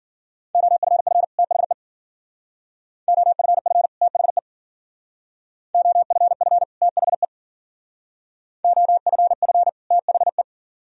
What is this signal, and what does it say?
Morse code, heard as a keyed tone, about 52 wpm, 700 Hz. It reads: OFF THE OFF THE OFF THE OFF THE